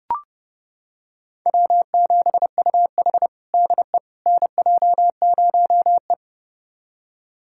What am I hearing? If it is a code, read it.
Morse code, heard as a keyed tone, about 30 wpm, 700 Hz. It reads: W7UH DE NJ0E
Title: USA Callsign DE Callsign - 4 Characters - 30wpm